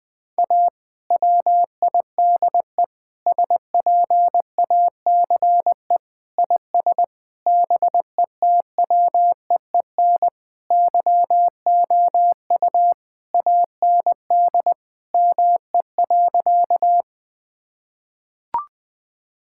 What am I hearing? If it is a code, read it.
Morse code, heard as a keyed tone, about 20 wpm, 700 Hz. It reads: A WIDE SPACE IS BETWEEN YOU AND ME.
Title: Sentences from Top 1000 Words - Review Part 6 - 20wpm